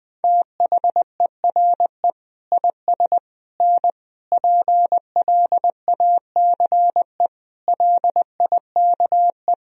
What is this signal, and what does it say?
Morse code, heard as a keyed tone, about 20 wpm, 700 Hz. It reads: THERE IS N PLACE LIKE
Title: Sentences from Top 200 Words and Abbreviations 20wpm